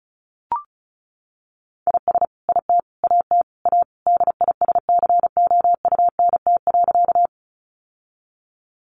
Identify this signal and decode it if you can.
Morse code, heard as a keyed tone, about 34 wpm, 700 Hz. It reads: IS IT AT A DISCOUNT.
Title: Sentences from Top 1200 Words 35wpm